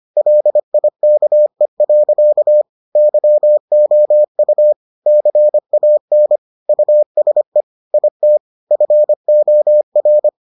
Morse code, heard as a keyed tone, about 25 wpm, 600 Hz. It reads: LIKE. YOU CAN USE IT FOR